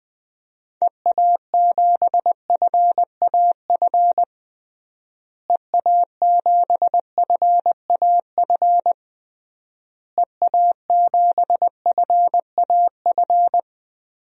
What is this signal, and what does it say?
Morse code, heard as a keyed tone, about 20 wpm, 700 Hz. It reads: EA7FAF EA7FAF EA7FAF
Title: DXCC-100 Call Signs - 6 Characters - Repeated 3x - 20wpm